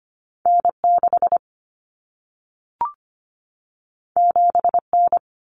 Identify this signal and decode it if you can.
Morse code, heard as a keyed tone, about 25 wpm, 700 Hz. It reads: N6 7N
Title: Common Cut Numbers - 2-Digit 25wpm